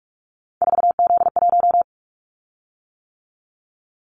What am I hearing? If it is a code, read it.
Morse code, heard as a keyed tone, about 44 wpm, 700 Hz. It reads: 4Z1
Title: Callsign Prefixes for Top 50 DXCC - 45wpm